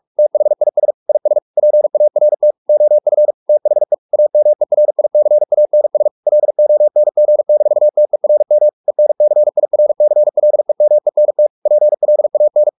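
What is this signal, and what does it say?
Morse code, heard as a keyed tone, about 44 wpm, 600 Hz. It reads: THIS IS PART OF THE AMERICANS LONG-TERM ENCIRCLEMENT PLAN
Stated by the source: Morse practice at 45 wpm, the novel Never by Ken Follett in English, from CW med Gnister.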